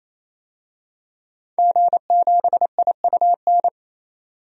Morse code, heard as a keyed tone, about 28 wpm, 700 Hz. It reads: G7IUN
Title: DXCC-100 Call Signs - 5 Characters - 28wpm